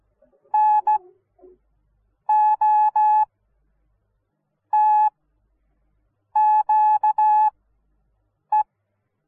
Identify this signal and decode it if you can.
Morse code, heard as a keyed tone, about 16 wpm, spaced as for 6 wpm, 850 Hz. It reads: NOTQE